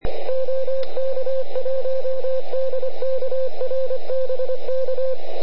Morse code, heard as a keyed tone, about 25 wpm, 520 Hz, with radio noise. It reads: OK1DKR BK
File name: ARCHIV / QRP2005 / ok1dkr